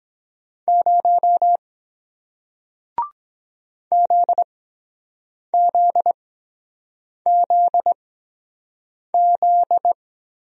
Morse code, heard as a keyed tone, about 23 wpm, spaced as for 7 wpm, 700 Hz. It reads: 0 ZZZZ